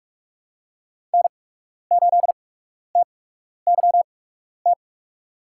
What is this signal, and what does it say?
Morse code, heard as a keyed tone, about 44 wpm, spaced as for 12 wpm, 700 Hz. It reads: N8TYT
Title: USA Call Signs - 5 Characters - 8x Character Spacing - 45wpm